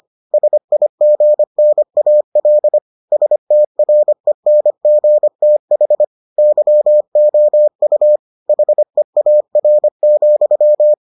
Morse code, heard as a keyed tone, about 25 wpm, 600 Hz. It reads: SIGNAL STRENGTH YOU HEAR,